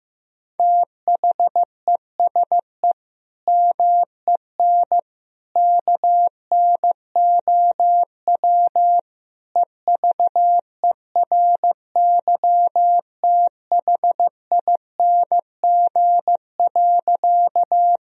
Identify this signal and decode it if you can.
Morse code, heard as a keyed tone, about 15 wpm, 700 Hz. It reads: THESE MEN KNOW EVERYTHING.
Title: Sentences from Top 1000 Words - Review Part 1 - 15wpm